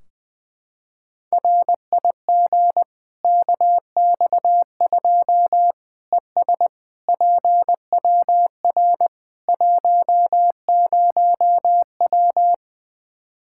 Morse code, heard as a keyed tone, about 20 wpm, 700 Hz. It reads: RIG KX2 ES PWR 10W
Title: K1USN SST QSO Practice 20wpm